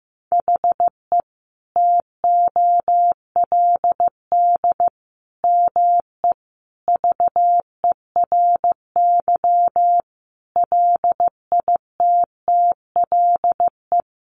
Morse code, heard as a keyed tone, about 15 wpm, 700 Hz. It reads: HE TOLD ME VERY LITTLE
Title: Sentences from Top 1500 Words 15wpm